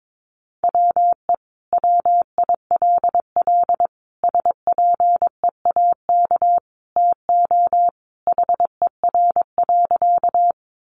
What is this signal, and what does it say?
Morse code, heard as a keyed tone, about 22 wpm, 700 Hz. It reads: WE WILL SPEAK TO HER.
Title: Sentences from Top 1100 Words 22wpm